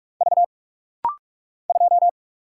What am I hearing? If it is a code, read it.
Morse code, heard as a keyed tone, about 44 wpm, 700 Hz. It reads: V 2
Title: Single Letter-Number Alt - Rapid-Fire 45wpm